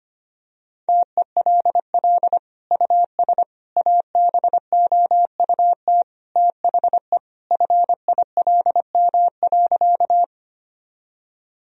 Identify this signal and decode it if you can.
Morse code, heard as a keyed tone, about 25 wpm, 700 Hz. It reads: TELL US ABOUT THE FILM.
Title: Sentences from Top 1000 Words - Review Part 3 - 25wpm